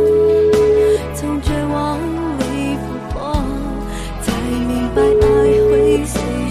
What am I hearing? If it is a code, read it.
Ringback tone (Europe, 425 Hz), with noise.